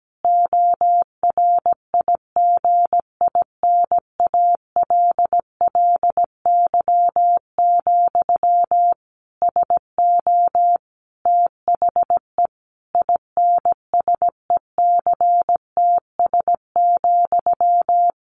Morse code, heard as a keyed tone, about 17 wpm, 700 Hz. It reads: ORIGINALLY, SO THE INSECTS,
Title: A Sound Of Thunder 17wpm